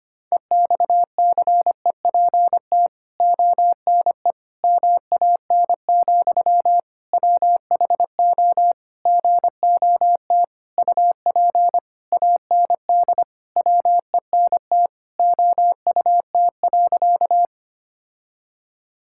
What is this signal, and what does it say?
Morse code, heard as a keyed tone, about 25 wpm, 700 Hz. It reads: EXCEPT ONE MAN, WHO GOT UP AND WENT OUT.
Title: Sentences from Top 1600 Words 25wpm